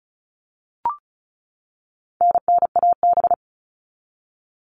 Morse code, heard as a keyed tone, about 34 wpm, 700 Hz. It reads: NNAB
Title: All Cut Numbers - 4-Digit 35wpm